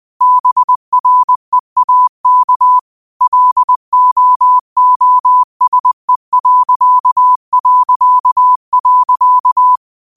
Morse code, heard as a keyed tone, about 20 wpm, 1000 Hz. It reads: BREAK LOOSE...
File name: qotdfiles / 20wpm